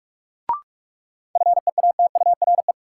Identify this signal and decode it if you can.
Morse code, heard as a keyed tone, about 47 wpm, 700 Hz. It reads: FEATURE
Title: Sentences from Top 800 Words - New Word Drill - 45wpm